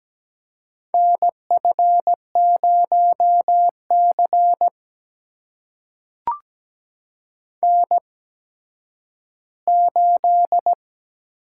Morse code, heard as a keyed tone, about 17 wpm, spaced as for 10 wpm, 700 Hz. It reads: NF0C N8
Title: USA Call Signs - 4 Characters - 8x Character Spacing - 17wpm